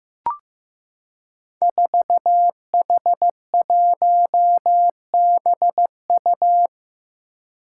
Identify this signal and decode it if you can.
Morse code, heard as a keyed tone, about 16 wpm, 700 Hz. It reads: 4H1BU
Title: Five Challenging Letter-Number Combinations 15wpm